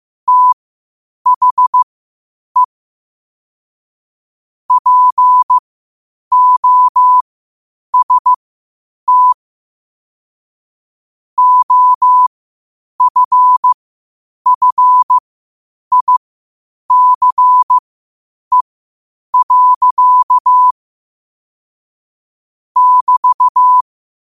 Morse code, heard as a keyed tone, about 15 wpm, spaced as for 8 wpm, 1000 Hz. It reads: THE POST OFFICE. <BT>
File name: qotdfiles / 8wpm